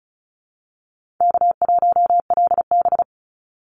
Morse code, heard as a keyed tone, about 34 wpm, 700 Hz. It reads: K1LB
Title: NA SOTA Chaser Callsigns 35wpm